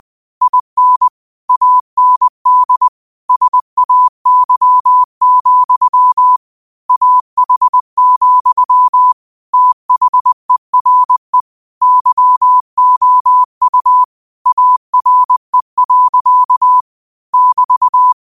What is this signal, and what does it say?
Morse code, heard as a keyed tone, about 20 wpm, 1000 Hz. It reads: IN AND SAY, AH, THERE YOU ARE. <BT>
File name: qotdfiles / 20wpm